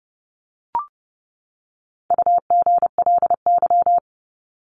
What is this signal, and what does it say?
Morse code, heard as a keyed tone, about 30 wpm, 700 Hz. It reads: UGLY
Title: Top 1751-2000 Words 30wpm